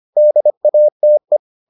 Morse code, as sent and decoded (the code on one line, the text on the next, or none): -.. .- - .
DATE